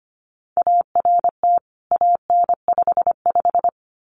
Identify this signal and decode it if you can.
Morse code, heard as a keyed tone, about 25 wpm, 700 Hz. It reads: ART AN55
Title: CWT Exchange 25wpm